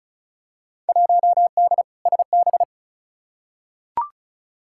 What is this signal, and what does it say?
Morse code, heard as a keyed tone, about 34 wpm, 700 Hz. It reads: 1D SB
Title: Field Day Exchanges - 35wpm